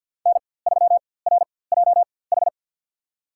Morse code, heard as a keyed tone, about 50 wpm, spaced as for 23 wpm, 700 Hz. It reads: N3RJH
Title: USA Call Signs - 5 Characters - 4x Character Spacing - 50wpm